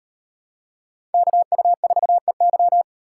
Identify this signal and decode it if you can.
Morse code, heard as a keyed tone, about 36 wpm, 700 Hz. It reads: KU4EY